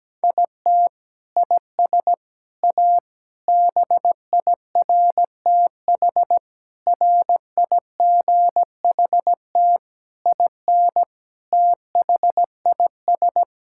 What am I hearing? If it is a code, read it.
Morse code, heard as a keyed tone, about 17 wpm, 700 Hz. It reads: IT IS A BIRTH RIGHT IN THIS
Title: Sentences from Top 1000 Words - Review Part 5 - 17wpm